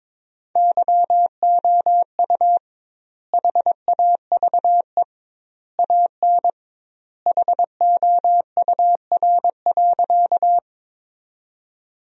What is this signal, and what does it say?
Morse code, heard as a keyed tone, about 22 wpm, 700 Hz. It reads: YOU HAVE AN HOUR.
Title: 2x Word-Spacing - Sentences from Top 400 Words - 22wpm